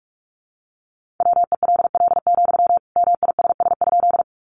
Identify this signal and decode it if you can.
Morse code, heard as a keyed tone, about 44 wpm, 700 Hz. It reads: WELL, MISS?
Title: Sentences from Top 500 Words 45wpm